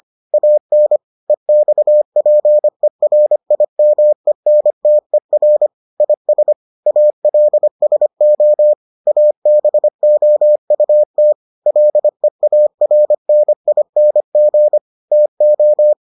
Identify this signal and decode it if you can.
Morse code, heard as a keyed tone, about 25 wpm, 600 Hz. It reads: AN EXPERIMENTER IS ALSO ABOUT LEARNING TO